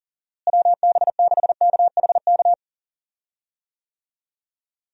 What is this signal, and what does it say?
Morse code, heard as a keyed tone, about 40 wpm, 700 Hz. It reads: WB6KHK
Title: USA Call Signs - 6 Characters - Repeated 3x - 40wpm